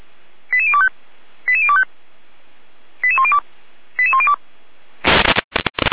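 Five-tone selective call (ZVEI), heard as quick runs of five tones, tones of about 70 ms. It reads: 88025 88025 88102 88102
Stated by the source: ZVEI five-tone selective call 88192 received by SDR, from the feuerwehr-orlamuende.org fire brigade audio archive.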